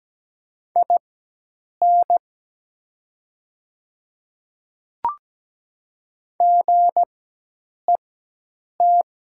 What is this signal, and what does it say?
Morse code, heard as a keyed tone, about 17 wpm, spaced as for 8 wpm, 700 Hz. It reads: IN GET
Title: 4x Character Spacing - Top 200 Words - 17wpm